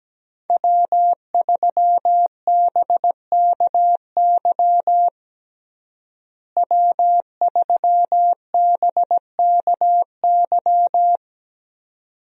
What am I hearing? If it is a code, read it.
Morse code, heard as a keyed tone, about 17 wpm, 700 Hz. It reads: W3BKY W3BKY
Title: USA Call Signs - 5 Characters - Repeated 2x - 17wpm